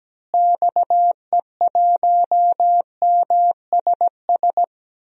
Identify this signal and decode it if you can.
Morse code, heard as a keyed tone, about 17 wpm, 700 Hz. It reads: XE1MSS
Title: DXCC-100 Call Signs - 6 Characters - Repeated 3x - 17wpm